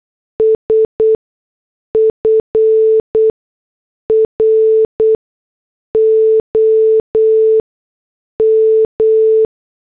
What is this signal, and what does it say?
Morse code, heard as a keyed tone, about 8 wpm, spaced as for 6 wpm, 440 Hz. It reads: S FROM